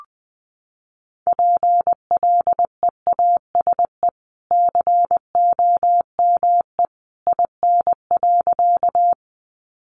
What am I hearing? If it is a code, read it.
Morse code, heard as a keyed tone, about 20 wpm, 700 Hz. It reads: PLEASE COME IN.
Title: Sentences from Top 1500 Words 20wpm